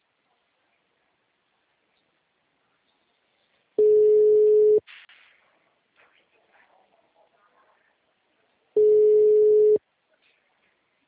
Ringback tone (Europe, 425 Hz).